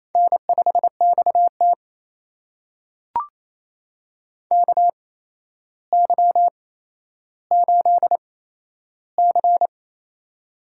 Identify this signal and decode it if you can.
Morse code, heard as a keyed tone, about 28 wpm, spaced as for 12 wpm, 700 Hz. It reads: N5XT KY8C
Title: USA Call Signs - 4 Characters - 8x Character Spacing - 28wpm